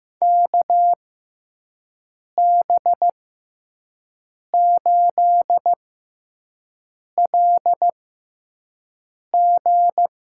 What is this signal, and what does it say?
Morse code, heard as a keyed tone, about 15 wpm, spaced as for 5 wpm, 700 Hz. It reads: KB8LG
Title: USA Call Signs - 6 Characters - 6x Character Spacing - 15wpm